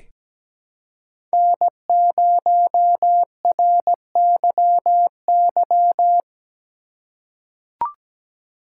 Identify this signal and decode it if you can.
Morse code, heard as a keyed tone, about 17 wpm, 700 Hz. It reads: N0RYY
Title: USA Call Signs - 5 Characters - 4x Character Spacing - 17wpm